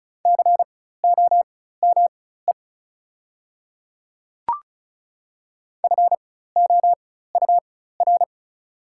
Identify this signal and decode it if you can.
Morse code, heard as a keyed tone, about 36 wpm, spaced as for 16 wpm, 690 Hz. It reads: COME FOUR
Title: 4x Character Spacing - Top 300 Words - 35wpm